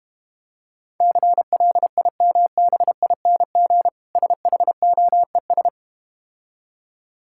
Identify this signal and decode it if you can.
Morse code, heard as a keyed tone, about 32 wpm, 700 Hz. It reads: CLIMBING SHOES